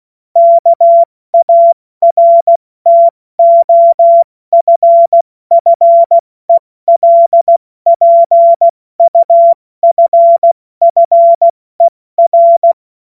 Morse code, heard as a keyed tone, about 16 wpm, 680 Hz. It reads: KARTOFFELPUFFER
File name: Morsekurs / Witze / Witz 27. Juni 2023 70 F80